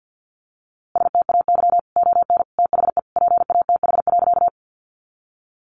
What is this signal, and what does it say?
Morse code, heard as a keyed tone, about 50 wpm, 700 Hz. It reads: STAY ON THE PATH.
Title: A Sound Of Thunder 50wpm